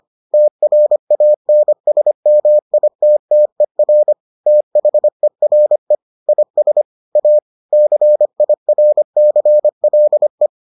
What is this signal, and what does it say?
Morse code, heard as a keyed tone, about 25 wpm, 600 Hz. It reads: TRANSMITTER THERE IS A CIRCLE